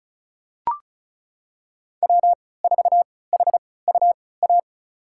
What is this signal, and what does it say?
Morse code, heard as a keyed tone, about 36 wpm, spaced as for 20 wpm, 690 Hz. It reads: W4HUA
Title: USA Call Signs - 5 Characters - 3x Character Spacing - 35wpm